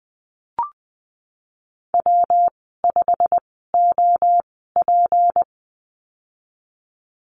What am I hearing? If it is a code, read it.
Morse code, heard as a keyed tone, about 20 wpm, spaced as for 15 wpm, 700 Hz. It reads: W5OP
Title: 167 Lesson 27g - Number 7 - Call Signs 20wpm